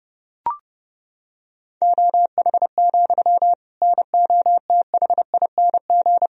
Morse code, heard as a keyed tone, about 30 wpm, 700 Hz. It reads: OH, NOTHING
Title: Sentences from Top 1000 Words - Review Part 3 - 30wpm